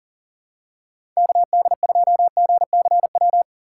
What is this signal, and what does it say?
Morse code, heard as a keyed tone, about 40 wpm, 700 Hz. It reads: KD2GCW
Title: USA Call Signs - 2x Character Spacing - 40wpm